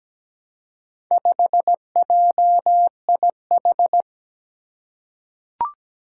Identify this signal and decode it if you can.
Morse code, heard as a keyed tone, about 17 wpm, 700 Hz. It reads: 5JIH
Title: Four Challenging Letter-Number Combinations 17wpm